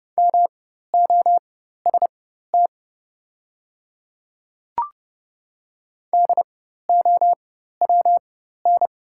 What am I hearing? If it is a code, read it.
Morse code, heard as a keyed tone, about 30 wpm, spaced as for 14 wpm, 700 Hz. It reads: MOST DOWN
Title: 4x Character Spacing - Top 200 Words - 30wpm